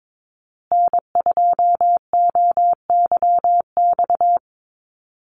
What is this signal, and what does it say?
Morse code, heard as a keyed tone, about 22 wpm, 700 Hz. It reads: N2OYX
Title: USA Call Signs Repeated 3x 22wpm